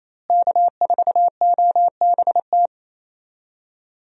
Morse code, heard as a keyed tone, about 28 wpm, 700 Hz. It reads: K4OBT
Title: USA Call Signs - 5 Characters - Repeated 3x - 28wpm